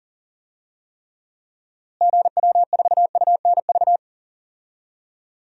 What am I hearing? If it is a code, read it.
Morse code, heard as a keyed tone, about 40 wpm, 700 Hz. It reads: GW4UNV
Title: DXCC-100 Call Signs - 6 Characters - Repeated 3x - 40wpm